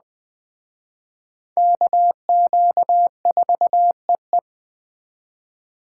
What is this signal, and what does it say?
Morse code, heard as a keyed tone, about 20 wpm, 700 Hz. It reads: KQ4EE